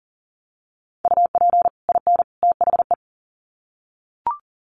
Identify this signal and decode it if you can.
Morse code, heard as a keyed tone, about 40 wpm, 700 Hz. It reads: UP IN THE
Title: Sets of 3 Words 40wpm